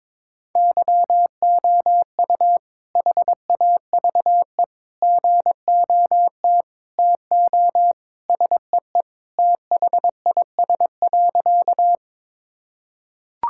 Morse code, heard as a keyed tone, about 21 wpm, 700 Hz. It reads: YOU HAVE GOT TO SEE THIS.
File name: practice / 22wpm / Sentences with 2-Word Phrases - 22wpm